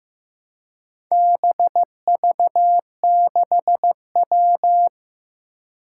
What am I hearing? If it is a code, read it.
Morse code, heard as a keyed tone, about 15 wpm, 700 Hz. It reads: BV6W
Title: Four Challenging Letter-Number Combinations 15wpm